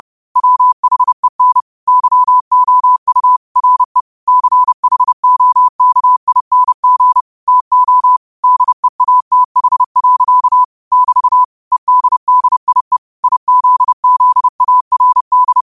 Morse code, heard as a keyed tone, about 30 wpm, 1000 Hz. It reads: WHEN YOU RE CHOKING TO DEATH. <BT> EDDIE IZZARD
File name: qotdfiles / 30wpm